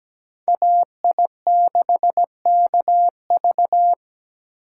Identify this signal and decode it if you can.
Morse code, heard as a keyed tone, about 20 wpm, 700 Hz. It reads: AI6KV
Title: Speed Racing - USA 2x2 Call Signs 17wpm